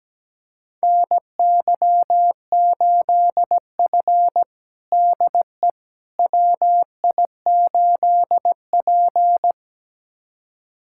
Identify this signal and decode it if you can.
Morse code, heard as a keyed tone, about 17 wpm, 700 Hz. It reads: NY8F DE WI8P
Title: USA Callsign DE Callsign - 4 Characters - 17wpm